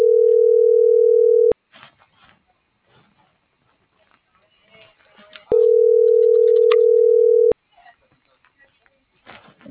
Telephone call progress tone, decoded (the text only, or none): ringback tone